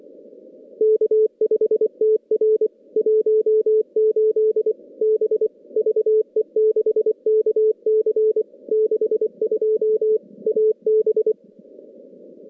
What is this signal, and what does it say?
Morse code, heard as a keyed tone, about 23 wpm, 410 Hz. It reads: K5TR 18 B VE6KC 62 AB
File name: contests / CW-SS-k5tr- -2100